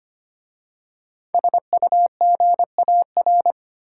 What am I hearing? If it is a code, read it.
Morse code, heard as a keyed tone, about 26 wpm, 690 Hz. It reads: SUGAR
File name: practice / 25wpm / Easy Words with Hints - 25wpm